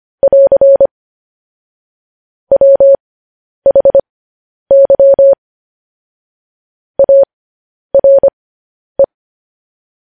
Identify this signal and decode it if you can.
Morse code, heard as a keyed tone, about 25 wpm, spaced as for 10 wpm, 550 Hz. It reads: <AR> WHY ARE
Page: news-headlines-in-morse-code-at-10-wpm